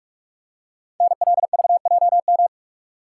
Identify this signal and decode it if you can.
Morse code, heard as a keyed tone, about 44 wpm, 690 Hz. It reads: NLVJM
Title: Code Groups - 5 Letters - 45wpm